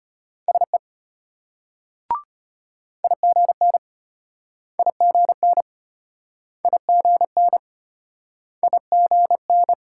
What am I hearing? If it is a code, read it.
Morse code, heard as a keyed tone, about 34 wpm, 700 Hz. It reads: SE IGN IGN IGN IGN